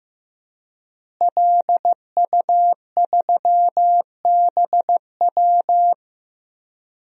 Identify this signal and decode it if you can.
Morse code, heard as a keyed tone, about 15 wpm, 700 Hz. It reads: LU3BW
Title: DXCC-100 Call Signs - 5 Characters - Repeated 3x - 15wpm